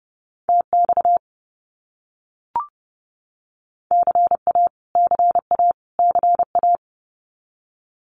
Morse code, heard as a keyed tone, about 30 wpm, 700 Hz. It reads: TX CA CA CA
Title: US States Repeated 3x 30wpm